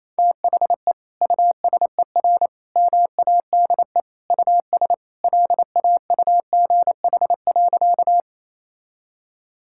Morse code, heard as a keyed tone, about 28 wpm, 700 Hz. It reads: THE USER MADE US LAUGH.
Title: Sentences from Top 1000 Words - Review Part 5 - 28wpm